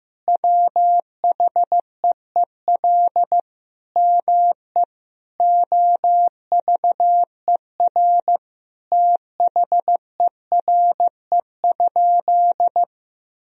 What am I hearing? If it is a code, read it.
Morse code, heard as a keyed tone, about 15 wpm, 700 Hz. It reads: WHEEL ME OVER THERE?
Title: Sentences from Top 1000 Words - Review Part 3 - 15wpm